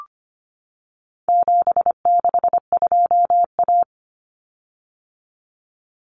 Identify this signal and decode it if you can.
Morse code, heard as a keyed tone, about 25 wpm, 700 Hz. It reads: 762A
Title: Common Cut Numbers - 4-Digit 25wpm